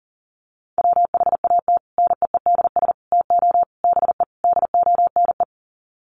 Morse code, heard as a keyed tone, about 40 wpm, 700 Hz. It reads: WHAT NEEDS TO BE DONE
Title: Sets of 5 Words Encore 40wpm